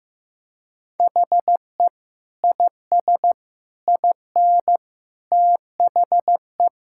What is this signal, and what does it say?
Morse code, heard as a keyed tone, about 16 wpm, 700 Hz. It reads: HE IS IN THE
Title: Sentences from Top 1600 Words 15wpm